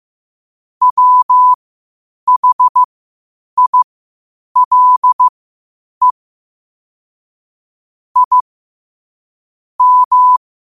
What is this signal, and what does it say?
Morse code, heard as a keyed tone, about 15 wpm, spaced as for 8 wpm, 1000 Hz. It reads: WHILE I M
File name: qotdfiles / 8wpm